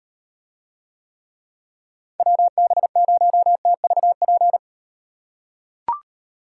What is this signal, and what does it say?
Morse code, heard as a keyed tone, about 38 wpm, 700 Hz. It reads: WB0TVP